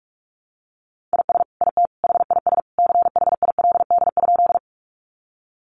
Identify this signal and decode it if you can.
Morse code, heard as a keyed tone, about 44 wpm, 700 Hz. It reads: IS IT HIS CHILD?
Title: Sentences from Top 800 Words 45wpm